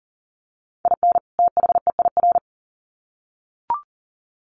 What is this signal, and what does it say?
Morse code, heard as a keyed tone, about 40 wpm, 700 Hz. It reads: IN THEIR
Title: Sets of 2 Words 40wpm